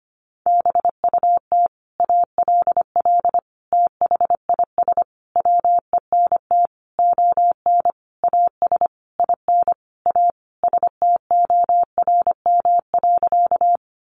Morse code, heard as a keyed tone, about 25 wpm, 700 Hz. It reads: BUT ALL THIS WENT ON AS IN A STORM.
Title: Sentences from Top 1500 Words 25wpm